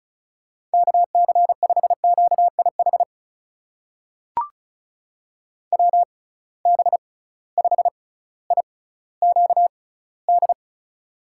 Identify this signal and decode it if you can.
Morse code, heard as a keyed tone, about 36 wpm, spaced as for 18 wpm, 700 Hz. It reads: KC5QIH WB5IQD
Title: USA Call Signs - 6 Characters - 6x Character Spacing - 35wpm